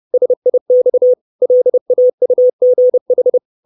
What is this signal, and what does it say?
Morse code, heard as a keyed tone, about 30 wpm, 500 Hz. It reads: SIX LAUGH
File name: 632 Words 30WPM